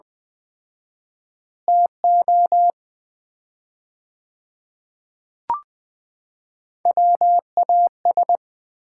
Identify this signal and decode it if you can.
Morse code, heard as a keyed tone, about 20 wpm, 700 Hz. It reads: TO WAS
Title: 4x Word-Spacing - Sets of 4 Words - 20wpm